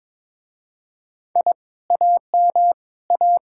Morse code, heard as a keyed tone, about 25 wpm, 700 Hz. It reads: I AM A
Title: Speed-Racing Common QSO Phrases - 22wpm